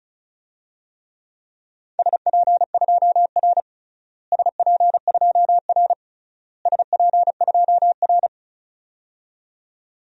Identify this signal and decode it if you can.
Morse code, heard as a keyed tone, about 36 wpm, 700 Hz. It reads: SP2R SP2R SP2R
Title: DXCC-100 Call Signs - 4 Characters - Repeated 3x - 35wpm